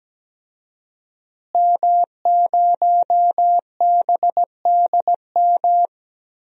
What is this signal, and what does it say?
Morse code, heard as a keyed tone, about 17 wpm, 700 Hz. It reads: M0BDM
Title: DXCC-100 Call Signs - 5 Characters - Repeated 3x - 17wpm